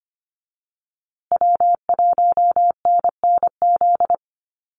Morse code, heard as a keyed tone, about 25 wpm, 700 Hz. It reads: W1NNZ
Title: USA Call Signs Repeated 3x 25wpm